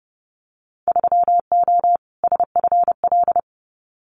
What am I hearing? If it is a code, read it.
Morse code, heard as a keyed tone, about 30 wpm, 700 Hz. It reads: 3O SFL
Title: Winter Field Day Exchanges 30wpm